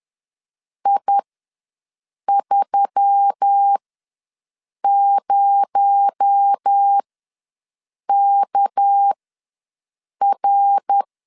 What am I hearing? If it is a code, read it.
Morse code, heard as a keyed tone, about 10 wpm, spaced as for 6 wpm, 800 Hz. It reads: I30KR
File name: LEZIONI / 6 PKX0 / 6 PKX0 ws 10wpm 4220